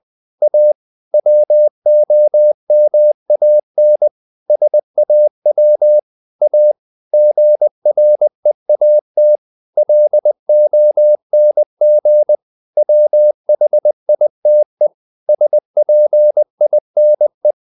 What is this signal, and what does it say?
Morse code, heard as a keyed tone, about 20 wpm, 600 Hz. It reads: A WOMAN SAW A GREAT LONG WHITE SPINE
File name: the old man and the sea part 2 20wpm